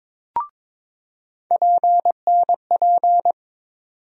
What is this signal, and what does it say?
Morse code, heard as a keyed tone, about 22 wpm, 700 Hz. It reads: PNP
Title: USA Call Sign Postfixes - 3 Letters 22wpm